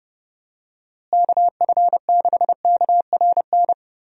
Morse code, heard as a keyed tone, about 30 wpm, 700 Hz. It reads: KF6KRN
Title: USA Call Signs - 6 Characters - Repeated 2x - 30wpm